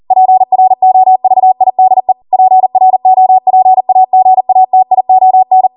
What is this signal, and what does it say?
Morse code, heard as a keyed tone, about 40 wpm, 750 Hz. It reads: PROVIDE PROPAGATION